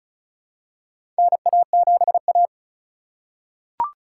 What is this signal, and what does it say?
Morse code, heard as a keyed tone, about 34 wpm, 700 Hz. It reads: NA7A
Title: USA Call Signs - 4 Characters - 4x Character Spacing - 35wpm